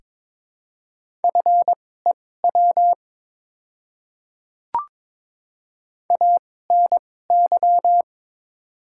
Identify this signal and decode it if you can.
Morse code, heard as a keyed tone, about 22 wpm, spaced as for 16 wpm, 700 Hz. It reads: FEW ANY
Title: 2x Character Spacing - Top 300 Words - 22wpm